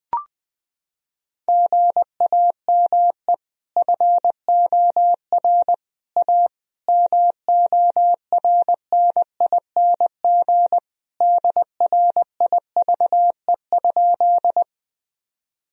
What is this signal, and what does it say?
Morse code, heard as a keyed tone, about 21 wpm, 700 Hz. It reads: GAME FOR A MORNING DRIVE?
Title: Sentences from Top 1000 Words - Review Part 3 - 20wpm